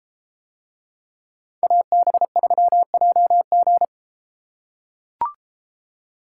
Morse code, heard as a keyed tone, about 31 wpm, 700 Hz. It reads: AB3JG